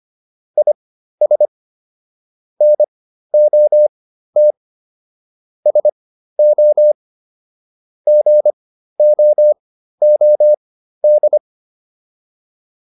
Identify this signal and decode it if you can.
Morse code, heard as a keyed tone, about 25 wpm, spaced as for 13 wpm, 600 Hz. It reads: IS NOT SO GOOD
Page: intermediate-practice-files